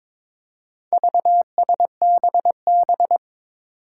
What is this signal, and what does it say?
Morse code, heard as a keyed tone, about 22 wpm, 700 Hz. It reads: VSBB
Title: Four Challenging Letter-Number Combinations 22wpm